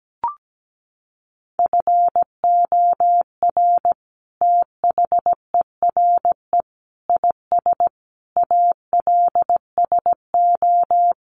Morse code, heard as a keyed tone, about 17 wpm, 700 Hz. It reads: FOR THERE IS ALSO
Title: Sentences from Top 1400 Words 17wpm